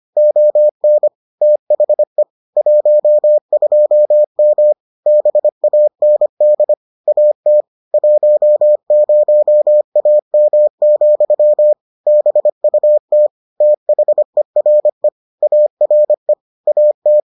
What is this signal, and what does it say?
Morse code, heard as a keyed tone, about 25 wpm, 600 Hz. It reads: ON THE 12M BAND AT 10AM, BUT THERE ARE AT